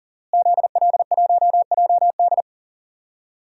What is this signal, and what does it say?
Morse code, heard as a keyed tone, about 40 wpm, 700 Hz. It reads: ZL1JD
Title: DXCC-100 Call Signs - 5 Characters - Repeated 3x - 40wpm